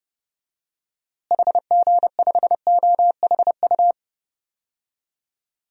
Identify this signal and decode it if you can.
Morse code, heard as a keyed tone, about 31 wpm, 700 Hz. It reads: HG5OHU
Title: DXCC-100 Call Signs - 6 Characters - Repeated 3x - 30wpm